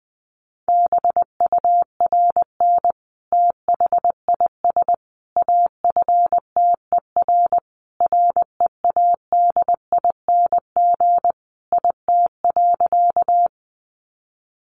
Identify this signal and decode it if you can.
Morse code, heard as a keyed tone, about 20 wpm, 700 Hz. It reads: BURN THIS AFTER READING IT.
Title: Sentences from Top 1300 Words 20wpm